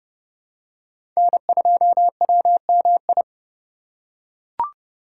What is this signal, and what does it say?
Morse code, heard as a keyed tone, about 32 wpm, 700 Hz. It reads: N2WMI
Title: USA Call Signs - 5 Characters - 4x Character Spacing - 30wpm